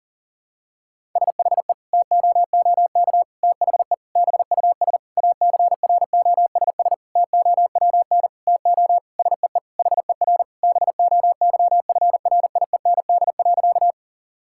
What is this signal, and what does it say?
Morse code, heard as a keyed tone, about 40 wpm, 700 Hz. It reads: SHE TOOK THE BUS ACROSS TOWN TO SEE HER BOYFRIEND.
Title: Sentences from Top 1000 Words - Review Part 6 - 40wpm